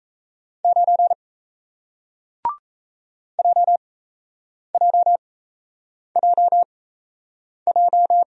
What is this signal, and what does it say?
Morse code, heard as a keyed tone, about 37 wpm, spaced as for 8 wpm, 700 Hz. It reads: 9 JJJJ